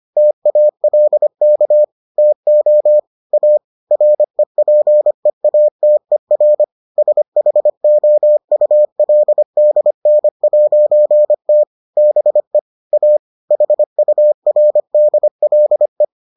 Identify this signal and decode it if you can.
Morse code, heard as a keyed tone, about 25 wpm, 600 Hz. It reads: TALK TO A REPEATER SHOULDN'T BE A HURDLE